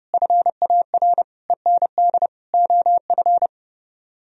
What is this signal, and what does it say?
Morse code, heard as a keyed tone, about 36 wpm, 700 Hz. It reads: FAR END OF
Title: Speed-Racing - Sets of 5 Words 30wpm